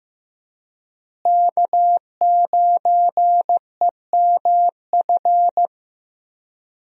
Morse code, heard as a keyed tone, about 15 wpm, 700 Hz. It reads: K9EMF
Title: USA Call Signs - 5 Characters - Repeated 2x - 15wpm